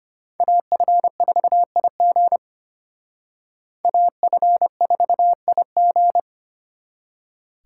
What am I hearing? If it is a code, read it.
Morse code, heard as a keyed tone, about 29 wpm, 700 Hz. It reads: AF4IG AF4IG
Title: Speed Racing - USA Call Signs 25wpm